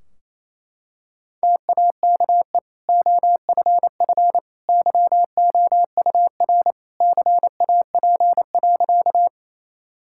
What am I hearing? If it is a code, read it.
Morse code, heard as a keyed tone, about 27 wpm, 700 Hz. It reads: TAKE OFF YOUR CAP.
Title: Sentences from Top 1600 Words 28wpm